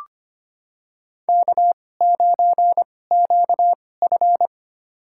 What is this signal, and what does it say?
Morse code, heard as a keyed tone, about 25 wpm, spaced as for 18 wpm, 700 Hz. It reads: K9QF
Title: USA Call Signs - 2x Character Spacing - 25wpm